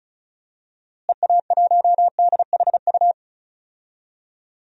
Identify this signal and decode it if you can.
Morse code, heard as a keyed tone, about 36 wpm, 700 Hz. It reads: EA1DHU
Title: DXCC-100 Call Signs - 6 Characters - Repeated 3x - 35wpm